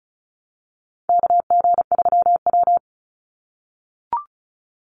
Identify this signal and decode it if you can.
Morse code, heard as a keyed tone, about 34 wpm, 700 Hz. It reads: KG3W
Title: NA SOTA Chaser Callsigns 35wpm